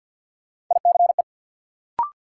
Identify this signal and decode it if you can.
Morse code, heard as a keyed tone, about 47 wpm, 700 Hz. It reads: ICE
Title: Sentences from Top 1000 Words - New Word Drill - 50wpm